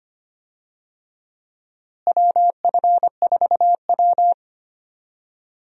Morse code, heard as a keyed tone, about 25 wpm, 700 Hz. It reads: WF4W
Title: USA Call Signs - 4 Characters - Repeated 2x - 25wpm